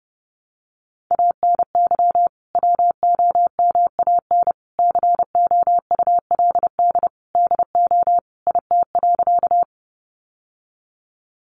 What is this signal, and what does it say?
Morse code, heard as a keyed tone, about 28 wpm, 700 Hz. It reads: ANY WOMAN COULD DO IT.
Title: Sentences from Top 1100 Words 30wpm